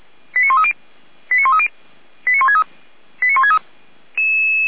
Five-tone selective call (ZVEI), heard as quick runs of five tones, tones of about 70 ms. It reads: 89130 89130 89163 89163
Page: SDR/ZVEI